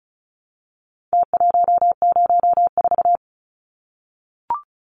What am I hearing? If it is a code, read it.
Morse code, heard as a keyed tone, about 34 wpm, 700 Hz. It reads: T104
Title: Numbers - Contest Style - 4-Digit 35wpm